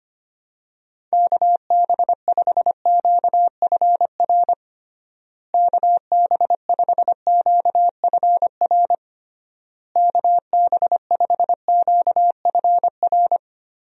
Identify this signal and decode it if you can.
Morse code, heard as a keyed tone, about 25 wpm, 700 Hz. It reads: KB5QFR KB5QFR KB5QFR
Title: USA Call Signs - 6 Characters - Repeated 3x - 25wpm